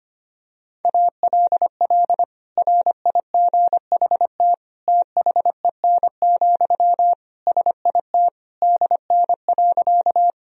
Morse code, heard as a keyed tone, about 25 wpm, 700 Hz. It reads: ALL RIGHT THEN, SIT DN.
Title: Sentences from Top 500 Words and Abbreviations 25wpm